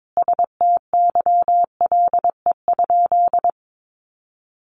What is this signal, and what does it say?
Morse code, heard as a keyed tone, about 22 wpm, 700 Hz. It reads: STYLE?
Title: Sentences from Top 800 Words 22wpm